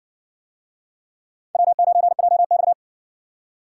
Morse code, heard as a keyed tone, about 60 wpm, 700 Hz. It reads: W9JX